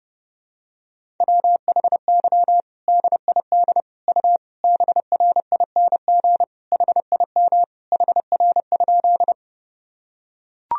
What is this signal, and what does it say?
Morse code, heard as a keyed tone, about 32 wpm, 700 Hz. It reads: WHY DID U BRING HIM HR?
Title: Sentences from Top 500 Words and Abbreviations 30wpm